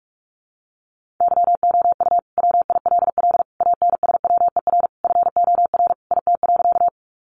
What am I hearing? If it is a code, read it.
Morse code, heard as a keyed tone, about 44 wpm, 700 Hz. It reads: YOU WILL ANSWER FOR IT.
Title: Sentences from Top 300 Words 45wpm